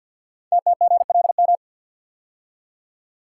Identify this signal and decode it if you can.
Morse code, heard as a keyed tone, about 52 wpm, 700 Hz. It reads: TTGLM
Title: Code Groups - 5 Letters - 50wpm